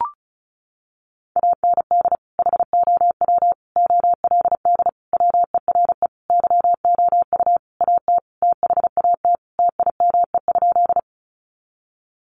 Morse code, heard as a keyed tone, about 34 wpm, 700 Hz. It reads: AND HOW OLD WERE YOU AT THAT TIME?
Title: Sentences from Top 200 Words 35wpm